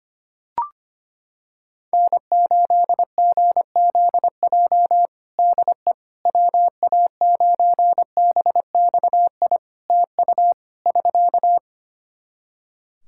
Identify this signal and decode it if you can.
Morse code, heard as a keyed tone, about 25 wpm, 700 Hz. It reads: N8GZJ DE WA9BXI TU <SK>
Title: K1USN SST QSO Practice 25wpm